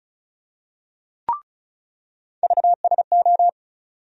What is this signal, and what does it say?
Morse code, heard as a keyed tone, about 36 wpm, 700 Hz. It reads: VSO
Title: USA Call Sign Postfixes - 3 Letters 35wpm